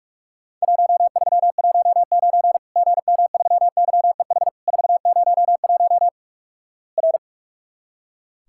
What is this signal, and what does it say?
Morse code, heard as a keyed tone, about 46 wpm, 700 Hz. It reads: 1319 GM3YEH 401 R
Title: Worked All Europe DX Contest - QTC Exchange - 45wpm